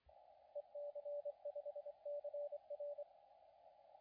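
Morse code, heard as a keyed tone, about 25 wpm, 580 Hz, with radio noise. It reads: EC5CR
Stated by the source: CW contest audio recorded at LA4C during the SAC CW contest.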